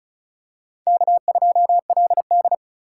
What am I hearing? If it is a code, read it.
Morse code, heard as a keyed tone, about 34 wpm, 700 Hz. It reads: K2LD
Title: USA Call Signs - 4 Characters - 6x Character Spacing - 35wpm